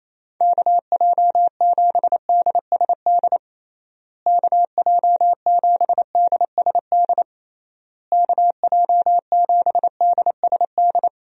Morse code, heard as a keyed tone, about 28 wpm, 700 Hz. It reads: KJ7DSD KJ7DSD KJ7DSD
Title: USA Call Signs - 6 Characters - Repeated 3x - 28wpm